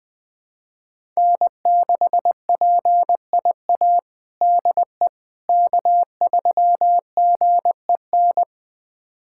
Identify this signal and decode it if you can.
Morse code, heard as a keyed tone, about 21 wpm, 700 Hz. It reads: N6PIA DE K3GEN
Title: USA Callsign DE Callsign - 5 Characters - 20wpm